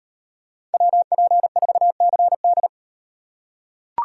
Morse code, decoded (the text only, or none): WP4CD